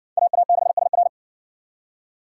Morse code, heard as a keyed tone, about 55 wpm, 700 Hz. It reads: AA6SR